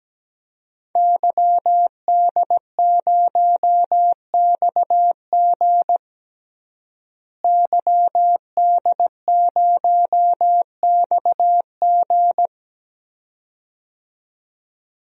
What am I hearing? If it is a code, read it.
Morse code, heard as a keyed tone, about 17 wpm, 700 Hz. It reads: YD0XG YD0XG
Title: DXCC-100 Call Signs - 5 Characters - Repeated 2x - 17wpm